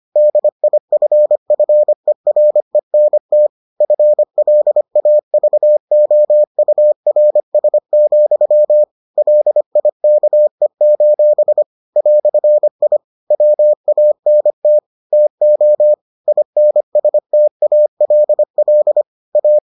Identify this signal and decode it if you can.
Morse code, heard as a keyed tone, about 25 wpm, 600 Hz. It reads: DIFFERENT FLAVOURS, LIKE: "I WANT TO INSTALL A